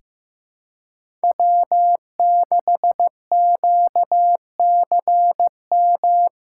Morse code, heard as a keyed tone, about 15 wpm, 700 Hz. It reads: W6QCM
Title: USA Call Signs - 5 Characters - 2x Character Spacing - 15wpm